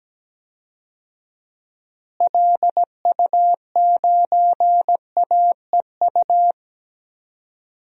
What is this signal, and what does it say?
Morse code, heard as a keyed tone, about 17 wpm, 700 Hz. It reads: LU9AEU
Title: DXCC-100 Call Signs - 6 Characters - Repeated 2x - 17wpm